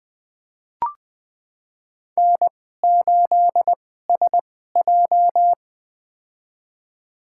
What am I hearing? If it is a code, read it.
Morse code, heard as a keyed tone, about 21 wpm, spaced as for 15 wpm, 700 Hz. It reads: N8SJ
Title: USA Call Signs - 4 Characters - 2x Character Spacing - 20wpm